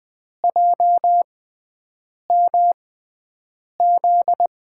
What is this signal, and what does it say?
Morse code, heard as a keyed tone, about 20 wpm, spaced as for 7 wpm, 700 Hz. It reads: JMZ
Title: USA Call Signs - 6 Characters - 6x Character Spacing - 20wpm